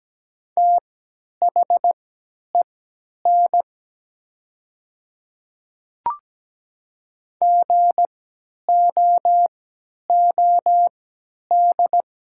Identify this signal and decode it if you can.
Morse code, heard as a keyed tone, about 17 wpm, spaced as for 10 wpm, 700 Hz. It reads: THEN GOOD
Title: 3x Character Spacing - Top 200 Words - 17wpm